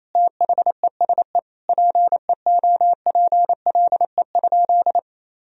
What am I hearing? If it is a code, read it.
Morse code, heard as a keyed tone, about 28 wpm, 700 Hz. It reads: THESE PEOPLE?
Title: Sentences from Top 1000 Words - Review Part 1 - 28wpm